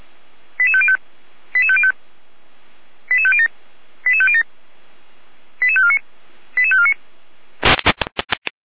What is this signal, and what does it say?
Five-tone selective call (ZVEI), heard as quick runs of five tones, tones of about 70 ms. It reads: 88505 88505 88507 88507 88549 88549
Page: ZVEI